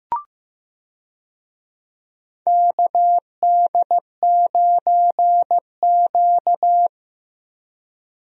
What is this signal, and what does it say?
Morse code, heard as a keyed tone, about 15 wpm, 700 Hz. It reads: KD9Q
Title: USA Call Signs - 4 Characters - Repeated 3x - 15wpm